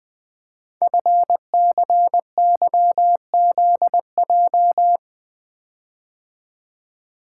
Morse code, heard as a keyed tone, about 20 wpm, 700 Hz. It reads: FCYZJ
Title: Code Groups - 5 Letters - 20wpm